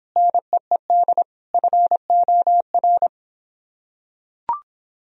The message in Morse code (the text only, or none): NEED FOR